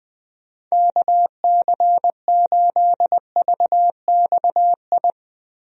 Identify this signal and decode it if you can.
Morse code, heard as a keyed tone, about 20 wpm, 700 Hz. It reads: KC8VXI
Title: USA Call Signs - 6 Characters - Repeated 3x - 20wpm